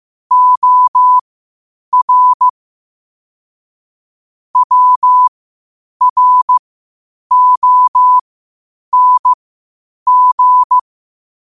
Morse code, heard as a keyed tone, about 15 wpm, spaced as for 8 wpm, 1000 Hz. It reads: OR WRONG
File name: qotdfiles / 8wpm